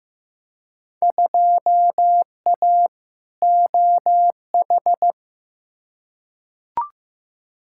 Morse code, heard as a keyed tone, about 15 wpm, 700 Hz. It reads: 2A OH
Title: Field Day Exchanges - 15wpm